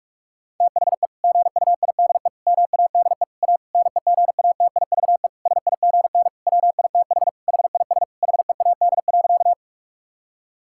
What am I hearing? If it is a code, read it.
Morse code, heard as a keyed tone, about 46 wpm, 700 Hz. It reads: THE GUIDE MADE A NEGATIVE SIGN WITH HIS HEAD.
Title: Sentences from Top 1000 Words - Review Part 6 - 45wpm